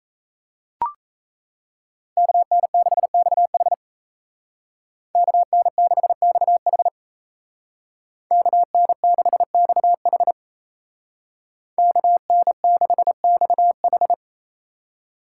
Morse code, heard as a keyed tone, about 36 wpm, 700 Hz. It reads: KN6XH KN6XH KN6XH KN6XH